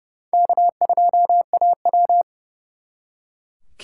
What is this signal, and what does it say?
Morse code, heard as a keyed tone, about 37 wpm, 700 Hz. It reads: K2AW
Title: Speed Racing - USA Call Signs 30wpm